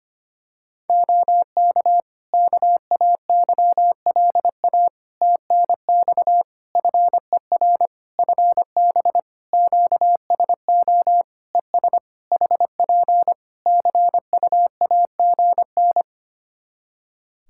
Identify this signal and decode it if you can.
Morse code, heard as a keyed tone, about 25 wpm, 700 Hz. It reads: OK KAYLA TNX FER FB QSO ES HP CUAGN
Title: K1USN SST QSO Practice 25wpm